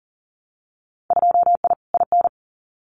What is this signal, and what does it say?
Morse code, heard as a keyed tone, about 40 wpm, 700 Hz. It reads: 2I IN
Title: Winter Field Day Exchanges Encore 40wpm